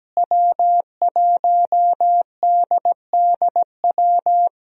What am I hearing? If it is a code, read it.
Morse code, heard as a keyed tone, about 17 wpm, 700 Hz. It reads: W1DDW
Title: USA Call Signs - 5 Characters - Repeated 2x - 17wpm